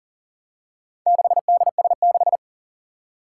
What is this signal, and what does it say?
Morse code, heard as a keyed tone, about 44 wpm, 700 Hz. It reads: 6DS6
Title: Four Challenging Letter-Number Combinations 40wpm